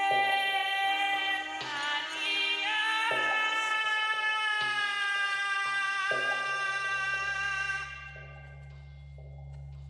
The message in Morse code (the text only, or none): none